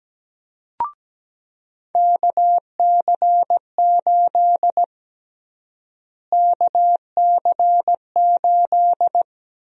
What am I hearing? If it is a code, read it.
Morse code, heard as a keyed tone, about 17 wpm, 700 Hz. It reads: KC8 KC8
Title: USA Call Sign Prefixes - Repeated 2x - 17wpm